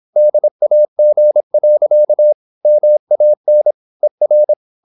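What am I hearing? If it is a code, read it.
Morse code, heard as a keyed tone, about 26 wpm, 600 Hz. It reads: DAG. MAN ER
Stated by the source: High-speed (QRQ) Morse at 26 wpm.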